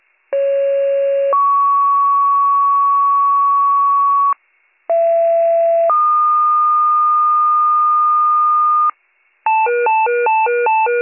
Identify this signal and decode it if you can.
Two-tone page, heard as a one-second tone, then a three-second tone, about 550 then 1120 Hz, about 670 then 1220 Hz. It is heard from 0.3 to 8.9 s.